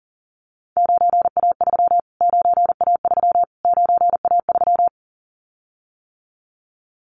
Morse code, heard as a keyed tone, about 40 wpm, 700 Hz. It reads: 9A3 9A3 9A3
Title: Callsign Prefixes for Top 25 DXCC - Repeated 3x - 40wpm